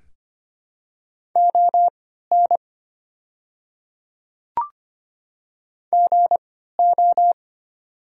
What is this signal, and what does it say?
Morse code, heard as a keyed tone, about 25 wpm, spaced as for 14 wpm, 700 Hz. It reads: ON GO
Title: 3x Character Spacing - Top 300 Words - 25wpm